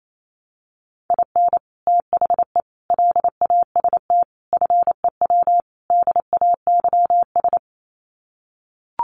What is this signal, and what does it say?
Morse code, heard as a keyed tone, about 28 wpm, 700 Hz. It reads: IN THE LAST FEW DAYS
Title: Sets of 5 Words Encore2 28wpm